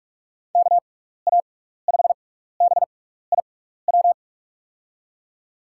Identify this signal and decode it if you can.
Morse code, heard as a keyed tone, about 45 wpm, spaced as for 15 wpm, 700 Hz. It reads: KA5BIW
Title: USA Call Signs - 6 Characters - 6x Character Spacing - 45wpm